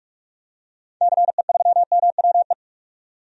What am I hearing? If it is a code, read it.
Morse code, heard as a keyed tone, about 47 wpm, 700 Hz. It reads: CE3MWE
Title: DXCC-100 Call Signs - 6 Characters - 45wpm